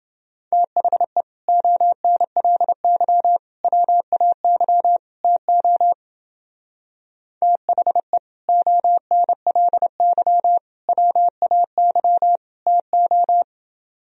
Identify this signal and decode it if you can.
Morse code, heard as a keyed tone, about 29 wpm, 700 Hz. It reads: THE ONLY WAY TO THE ONLY WAY TO